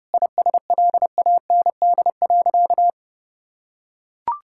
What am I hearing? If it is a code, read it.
Morse code, heard as a keyed tone, about 32 wpm, 700 Hz. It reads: ISLAND.
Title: Sentences from Top 1600 Words 30wpm